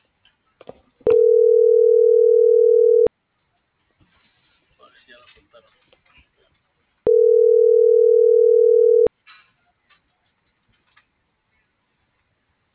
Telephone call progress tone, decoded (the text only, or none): ringback tone